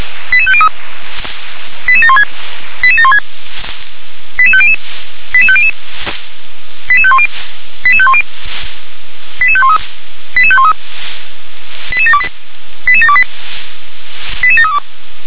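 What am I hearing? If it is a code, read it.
Five-tone selective call (ZVEI), heard as quick runs of five tones, tones of about 80 ms. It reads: 88503 88716 88716 88500 88500 88510 88510 88513 88513 88728 88728 88732
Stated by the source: ZVEI five-tone call 88758 received on SDR.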